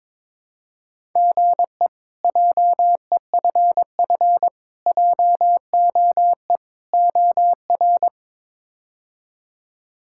Morse code, heard as a keyed tone, about 22 wpm, 700 Hz. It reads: GE JEFF JOE OR
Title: K1USN SST QSO Practice - 22wpm